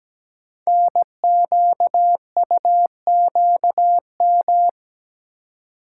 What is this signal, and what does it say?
Morse code, heard as a keyed tone, about 17 wpm, 700 Hz. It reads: NQUQM
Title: Code Groups - 5 Letters - 17wpm